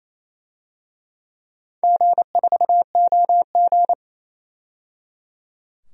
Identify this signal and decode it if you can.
Morse code, heard as a keyed tone, about 28 wpm, 700 Hz. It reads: G4OG
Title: DXCC-100 Call Signs - 4 Characters - 28wpm